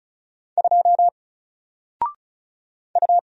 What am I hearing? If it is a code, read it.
Morse code, heard as a keyed tone, about 35 wpm, 700 Hz. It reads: 2 U
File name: Single-Letter-Number-35wpm